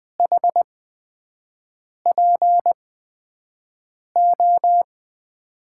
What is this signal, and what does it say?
Morse code, heard as a keyed tone, about 21 wpm, spaced as for 5 wpm, 700 Hz. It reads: HPO
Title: USA Call Signs - 8x Character Spacing - 20wpm